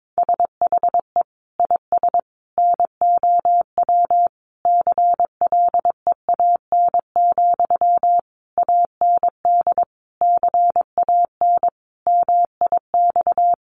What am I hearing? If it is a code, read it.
Morse code, heard as a keyed tone, about 22 wpm, 700 Hz. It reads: SHE IS NOW CLEAN, AND CAN MIX
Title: Sentences from Top 1100 Words 22wpm